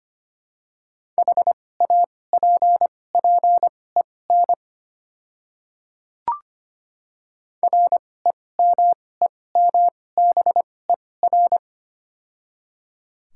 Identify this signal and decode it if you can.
Morse code, heard as a keyed tone, about 25 wpm, spaced as for 18 wpm, 700 Hz. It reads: HAPPEN REMEMBER
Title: 2x Character Spacing - Top 500 Words - 25wpm